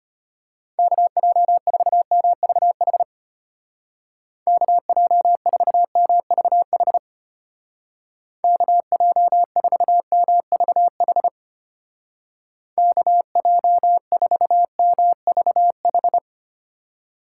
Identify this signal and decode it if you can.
Morse code, heard as a keyed tone, about 32 wpm, 700 Hz. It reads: KJ4MVH KJ4MVH KJ4MVH KJ4MVH